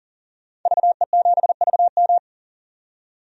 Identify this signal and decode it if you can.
Morse code, heard as a keyed tone, about 37 wpm, 700 Hz. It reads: VE7VM
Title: DXCC-100 Call Signs - 5 Characters - 40wpm